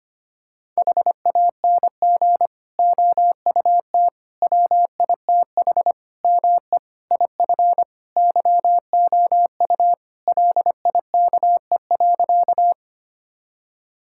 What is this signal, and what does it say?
Morse code, heard as a keyed tone, about 25 wpm, 700 Hz. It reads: HANG OUT WITH ME IF YOU LIKE.
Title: Sentences from Top 1600 Words 25wpm